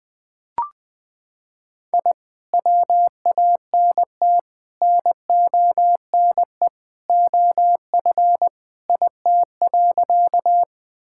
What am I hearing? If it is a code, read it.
Morse code, heard as a keyed tone, about 20 wpm, 700 Hz. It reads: I WANT NONE OF IT.
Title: Sentences with 2-Word Phrases - 20wpm